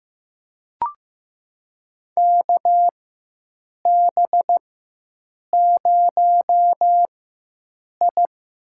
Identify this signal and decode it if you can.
Morse code, heard as a keyed tone, about 15 wpm, spaced as for 7 wpm, 700 Hz. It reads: KB0I
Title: USA Call Signs - 4x Character Spacing - 15wpm